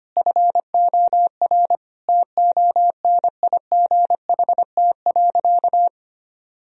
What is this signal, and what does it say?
Morse code, heard as a keyed tone, about 25 wpm, 700 Hz. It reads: FOR TONIGHT.
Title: Sentences from Top 1000 Words - Review Part 3 - 25wpm